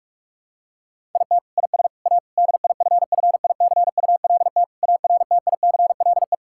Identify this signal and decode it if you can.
Morse code, heard as a keyed tone, about 46 wpm, 700 Hz. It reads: IT IS A DIFFICULT ARTICLE
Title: Sentences from Top 1000 Words - Review Part 5 - 45wpm